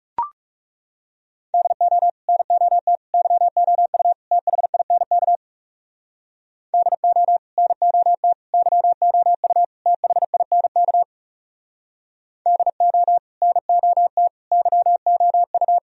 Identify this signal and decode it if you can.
Morse code, heard as a keyed tone, about 37 wpm, 700 Hz. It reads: DO NOT YOU THINK DO NOT YOU THINK DO NOT YOU
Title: Speed-Racing - Sets of 4 Words 30wpm